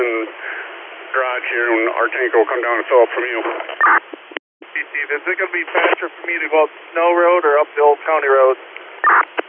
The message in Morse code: none

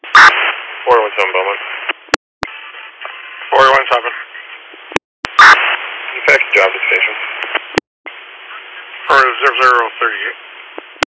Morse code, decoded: none

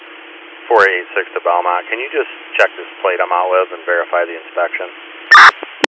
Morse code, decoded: none

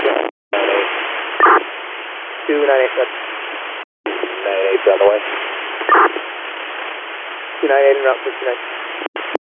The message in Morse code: none

